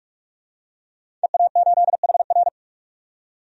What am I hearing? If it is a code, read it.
Morse code, heard as a keyed tone, about 47 wpm, 700 Hz. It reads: EA8HR
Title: DXCC-100 Call Signs - 5 Characters - 45wpm